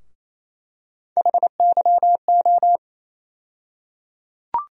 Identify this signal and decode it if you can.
Morse code, heard as a keyed tone, about 28 wpm, 700 Hz. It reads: HYO
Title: USA Call Sign Postfixes - 3 Letters 28wpm